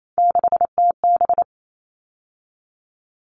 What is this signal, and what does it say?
Morse code, heard as a keyed tone, about 28 wpm, 700 Hz. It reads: 6TB
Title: All Cut Numbers - 3-Digit 28wpm